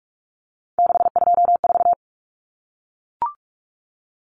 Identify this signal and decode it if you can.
Morse code, heard as a keyed tone, about 44 wpm, 700 Hz. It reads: 624
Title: Numbers - 3-Digit 45wpm